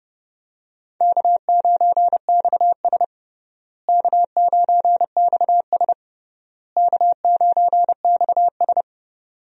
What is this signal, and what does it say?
Morse code, heard as a keyed tone, about 31 wpm, 700 Hz. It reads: K9XS K9XS K9XS
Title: USA Call Signs - 4 Characters - Repeated 3x - 30wpm